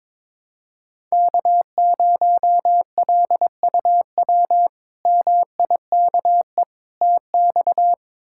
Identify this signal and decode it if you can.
Morse code, heard as a keyed tone, about 22 wpm, 700 Hz. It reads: K0LUW MIKE TX
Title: K1USN SST QSO Practice - 22wpm